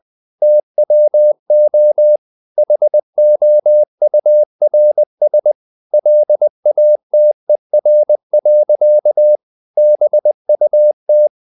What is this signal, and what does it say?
Morse code, heard as a keyed tone, about 20 wpm, 600 Hz. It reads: TWO HOURS LATER. BUT
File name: the old man and the sea part 2 20wpm